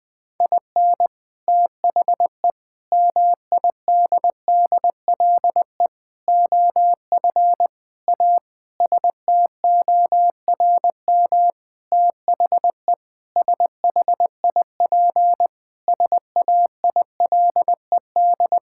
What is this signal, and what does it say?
Morse code, heard as a keyed tone, about 20 wpm, 700 Hz. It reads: IN THE MIDDLE OF A STORM THE SHIP SAILED
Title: Sentences with 5-Word Phrases - 20wpm